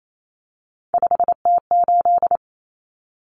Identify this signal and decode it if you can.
Morse code, heard as a keyed tone, about 28 wpm, 700 Hz. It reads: 5T8
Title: Common Cut Numbers - 3-Digit 28wpm